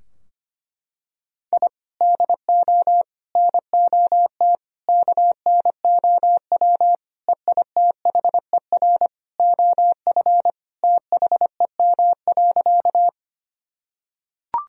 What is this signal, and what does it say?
Morse code, heard as a keyed tone, about 25 wpm, 700 Hz. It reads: I DO NOT KNOW EITHER OF THEM.
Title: Sentences from Top 1000 Words - Review Part 4 - 25wpm